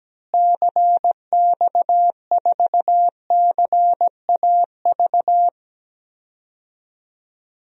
Morse code, heard as a keyed tone, about 17 wpm, 700 Hz. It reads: CX4CAV
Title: DXCC-100 Call Signs - 6 Characters - Repeated 3x - 17wpm